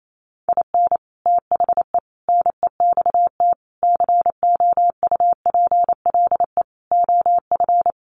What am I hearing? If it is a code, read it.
Morse code, heard as a keyed tone, about 28 wpm, 700 Hz. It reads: IN THE NEXT COUPLE OF
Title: Sets of 5 Words Encore2 28wpm